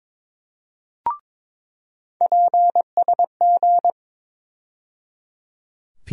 Morse code, heard as a keyed tone, about 22 wpm, 700 Hz. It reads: PSG
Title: USA Call Sign Postfixes - 3 Letters 22wpm